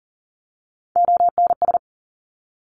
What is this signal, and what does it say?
Morse code, heard as a keyed tone, about 40 wpm, 700 Hz. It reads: ONS
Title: Top Three Letter Combinations in Words 40wpm